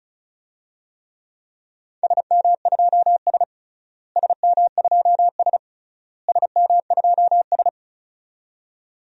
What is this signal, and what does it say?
Morse code, heard as a keyed tone, about 36 wpm, 700 Hz. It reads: SM2S SM2S SM2S
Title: DXCC-100 Call Signs - 4 Characters - Repeated 3x - 35wpm